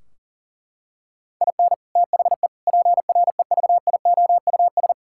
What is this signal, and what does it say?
Morse code, heard as a keyed tone, about 40 wpm, 700 Hz. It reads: IN THE PREVIOUS
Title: Sentences from Top 1600 Words 40wpm